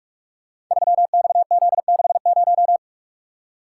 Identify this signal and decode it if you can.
Morse code, heard as a keyed tone, about 44 wpm, 700 Hz. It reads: 3XZ60
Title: Code Groups - 5 Letter:Numbers - 45wpm